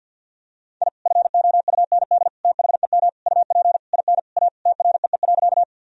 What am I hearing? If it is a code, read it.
Morse code, heard as a keyed tone, about 47 wpm, 700 Hz. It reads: I FOUND THEM UP IN A TREE.
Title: Sentences from Top 1000 Words - Review Part 1 - 50wpm